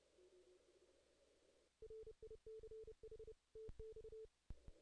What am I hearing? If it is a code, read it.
Morse code, heard as a keyed tone, about 28 wpm, 440 Hz. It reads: RICH TX
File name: 2016 JAN CW NAQP / 2016 Jan naqp k5ot - 0-9